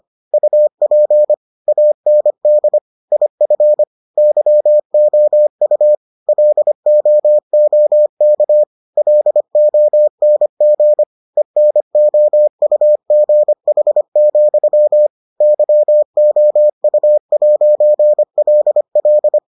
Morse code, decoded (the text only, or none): UP AND IF YOU LOOK LONG ENOUGH, YOU'LL